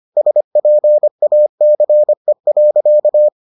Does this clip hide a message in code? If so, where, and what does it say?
Yes: Morse code from 0.2 to 3.3 s, heard as a keyed tone, about 25 wpm, 600 Hz. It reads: SPACE.